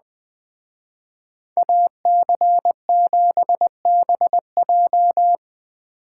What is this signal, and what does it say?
Morse code, heard as a keyed tone, about 20 wpm, 700 Hz. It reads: AC7BJ